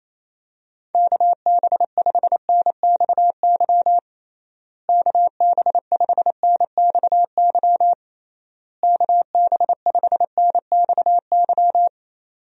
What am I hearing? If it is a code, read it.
Morse code, heard as a keyed tone, about 28 wpm, 700 Hz. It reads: KB5NXY KB5NXY KB5NXY
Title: USA Call Signs - 6 Characters - Repeated 3x - 28wpm